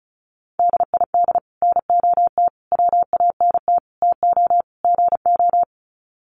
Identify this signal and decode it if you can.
Morse code, heard as a keyed tone, about 34 wpm, 700 Hz. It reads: DID NOT WANT TO GO
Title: Sets of 5 Words 35wpm